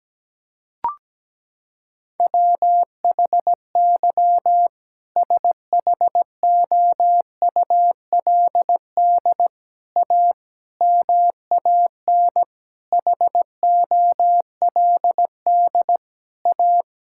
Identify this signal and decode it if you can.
Morse code, heard as a keyed tone, about 17 wpm, 700 Hz. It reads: WHY SHOULD A MAN HOLD A
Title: Sentences from Top 1600 Words 17wpm